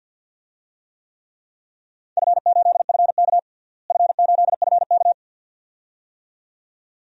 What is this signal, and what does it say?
Morse code, heard as a keyed tone, about 48 wpm, 700 Hz. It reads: F8FK F8FK
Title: DXCC-100 Call Signs - 4 Characters - Repeated 2x - 50wpm